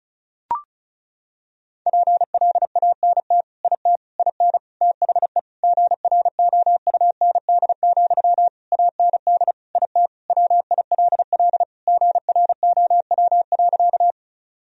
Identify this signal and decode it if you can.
Morse code, heard as a keyed tone, about 35 wpm, 700 Hz. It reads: PLANT IT IN THE GROUND, AND IT WILL GROW.
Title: Sentences from Top 1000 Words - Review Part 2 - 35wpm